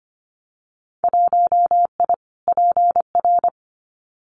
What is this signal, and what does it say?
Morse code, heard as a keyed tone, about 25 wpm, 700 Hz. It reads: 1I PR
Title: Winter Field Day Exchanges Encore 25wpm